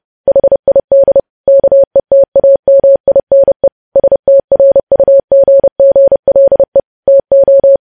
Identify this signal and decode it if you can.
Morse code, heard as a keyed tone, about 30 wpm, 550 Hz. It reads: HID KETAMINE STRUGGLE TO
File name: podcast / News headlines-1x-30x30